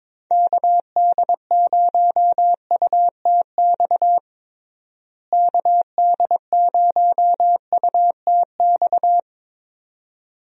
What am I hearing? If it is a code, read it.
Morse code, heard as a keyed tone, about 22 wpm, 700 Hz. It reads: KD0UTX KD0UTX
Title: USA Call Signs - 6 Characters - Repeated 2x - 22wpm